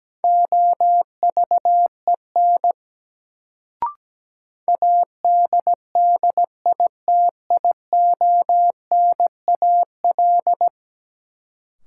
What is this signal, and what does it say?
Morse code, heard as a keyed tone, about 17 wpm, 700 Hz. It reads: OVEN ADDITIONAL
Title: Sentences from Top 1000 Words - Review Part 4 - New Word Drill - 17wpm